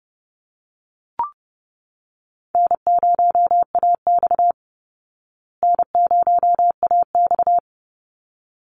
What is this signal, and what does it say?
Morse code, heard as a keyed tone, about 28 wpm, 700 Hz. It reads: N0AX N0AX
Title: Easy Call Signs Repeated 2x - 30wpm